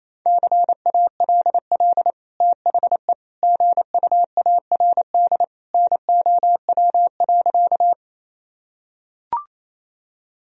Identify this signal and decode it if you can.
Morse code, heard as a keyed tone, about 28 wpm, 700 Hz. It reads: CALL THE GUARD NOW.
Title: Sentences from Top 1000 Words - Review Part 4 - 28wpm